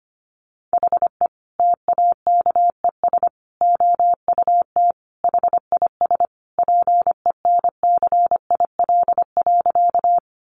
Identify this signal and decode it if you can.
Morse code, heard as a keyed tone, about 25 wpm, 700 Hz. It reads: HE TAKES OUT HIS PENCIL.
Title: Additional Easy Phrases 25wpm